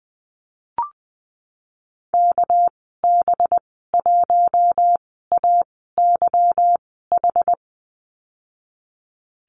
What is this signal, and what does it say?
Morse code, heard as a keyed tone, about 19 wpm, spaced as for 14 wpm, 700 Hz. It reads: KB1AYH
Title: 219 Lesson 35g - Number 0 - Call Signs 20wpm